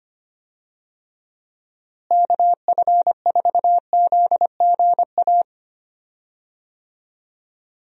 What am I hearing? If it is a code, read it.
Morse code, heard as a keyed tone, about 25 wpm, 700 Hz. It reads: KF4ZGA
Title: USA Call Signs - 6 Characters - Repeated 3x - 25wpm